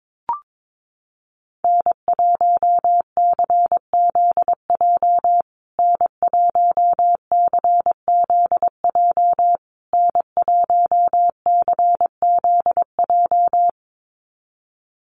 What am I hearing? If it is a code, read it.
Morse code, heard as a keyed tone, about 22 wpm, 700 Hz. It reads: N1CZJ N1CZJ N1CZJ
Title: USA Call Signs Repeated 3x 22wpm